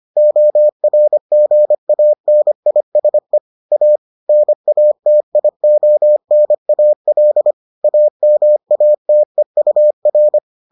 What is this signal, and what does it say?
Morse code, heard as a keyed tone, about 25 wpm, 600 Hz. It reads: ORGANISE A NATIONAL AMATEUR